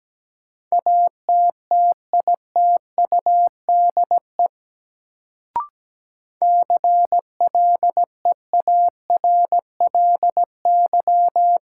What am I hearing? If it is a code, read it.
Morse code, heard as a keyed tone, about 17 wpm, 700 Hz. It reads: ATTITUDE CLEARLY
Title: Sentences from Top 1300 Words - New Word Drill - 17wpm